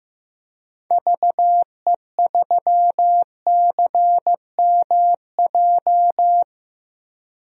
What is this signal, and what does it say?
Morse code, heard as a keyed tone, about 15 wpm, 700 Hz. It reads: VE3CMJ
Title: DXCC-100 Call Signs - 6 Characters - 15wpm